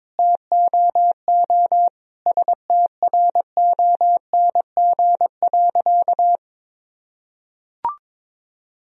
Morse code, heard as a keyed tone, about 22 wpm, 700 Hz. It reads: TOO STRONG.
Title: Sentences from Top 1000 Words - Review Part 4 - 22wpm